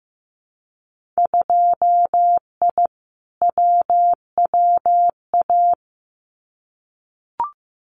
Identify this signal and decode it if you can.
Morse code, heard as a keyed tone, about 15 wpm, 700 Hz. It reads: 2I WWA
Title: Winter Field Day Exchanges Encore 15wpm